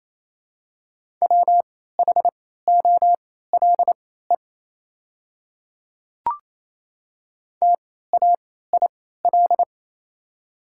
Morse code, heard as a keyed tone, about 28 wpm, spaced as for 16 wpm, 700 Hz. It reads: WHOLE TAIL
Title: 3x Character Spacing - Top 500 Words - 28wpm